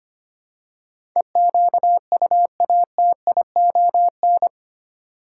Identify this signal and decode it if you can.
Morse code, heard as a keyed tone, about 25 wpm, 700 Hz. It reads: EQUATION
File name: Eight Letter Words - 25wpm